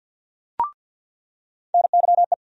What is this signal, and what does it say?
Morse code, heard as a keyed tone, about 48 wpm, 700 Hz. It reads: NYE
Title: USA Call Sign Postfixes - 3 Letters 50wpm